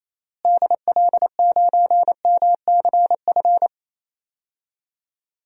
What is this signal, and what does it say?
Morse code, heard as a keyed tone, about 28 wpm, 700 Hz. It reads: DL9MCF
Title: DXCC-100 Call Signs - 6 Characters - Repeated 3x - 28wpm